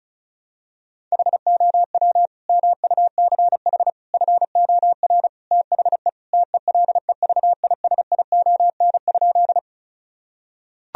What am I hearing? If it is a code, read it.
Morse code, heard as a keyed tone, about 36 wpm, 700 Hz. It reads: HOW MUCH FOR THE TELEVISION?
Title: Sentences from Top 1000 Words - Review Part 4 - 35wpm